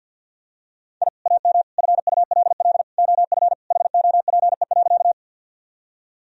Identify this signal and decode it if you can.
Morse code, heard as a keyed tone, about 50 wpm, 700 Hz. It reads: I AM FULL OF HOPE.
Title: Sentences from Top 1000 Words - Review Part 3 - 50wpm